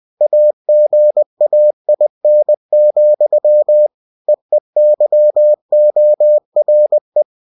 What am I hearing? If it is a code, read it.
Morse code, heard as a keyed tone, about 20 wpm, 600 Hz. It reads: AGAIN, EEYORE
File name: cw-books / winnie-the-pooh / winnie-20wpm-page-0032